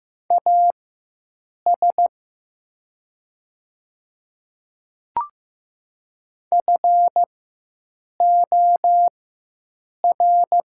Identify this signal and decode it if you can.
Morse code, heard as a keyed tone, about 15 wpm, spaced as for 7 wpm, 700 Hz. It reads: AS FOR
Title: 4x Character Spacing - Top 100 Words - 15wpm